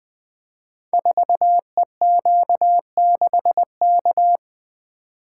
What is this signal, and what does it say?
Morse code, heard as a keyed tone, about 20 wpm, 700 Hz. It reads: 4EQ6K
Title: Code Groups - 5 Letter:Numbers - 20wpm